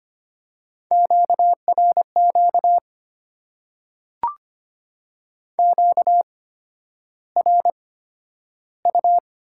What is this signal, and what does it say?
Morse code, heard as a keyed tone, about 25 wpm, spaced as for 11 wpm, 700 Hz. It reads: QRQ QRU
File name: Top 90 QSO Elements - 8x Character Spacing - 25wpm